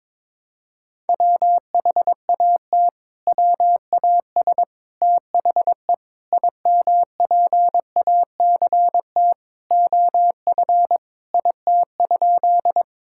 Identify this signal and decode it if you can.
Morse code, heard as a keyed tone, about 22 wpm, 700 Hz. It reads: WHAT WAS THE IMPACT OF IT?
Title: Sentences from Top 1000 Words - Review Part 6 - 22wpm